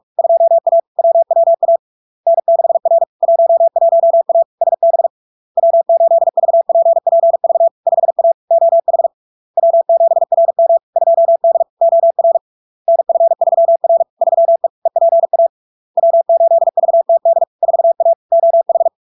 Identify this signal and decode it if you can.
Morse code, heard as a keyed tone, about 46 wpm, 670 Hz. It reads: 2A WWA N6R 11A SB W8VPPV 5A OH W7RM 2D OR NF3R 3E EPA W8VTD 4A OH
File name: LARK-FD-2022-Log-in-CW-45wpm-670-Hz